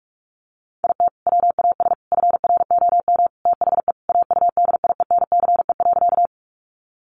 Morse code, heard as a keyed tone, about 44 wpm, 700 Hz. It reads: IT WAS FROM THE AUDIENCE.
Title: Sentences from Top 800 Words 45wpm